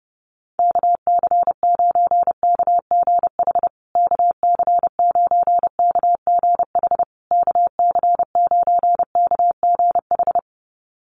Morse code, heard as a keyed tone, about 30 wpm, 700 Hz. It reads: KC9KGH KC9KGH KC9KGH
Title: USA Call Signs Repeated 3x 30wpm